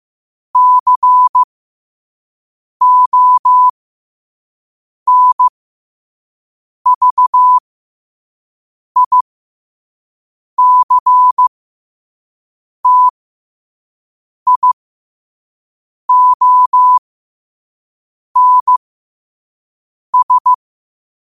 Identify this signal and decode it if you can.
Morse code, heard as a keyed tone, about 15 wpm, spaced as for 5 wpm, 1000 Hz. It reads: CONVICTIONS
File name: qotdfiles / 5wpm